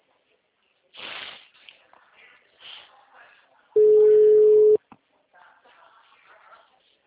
Ringback tone (Europe, 425 Hz).
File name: out- -2009- - - .62135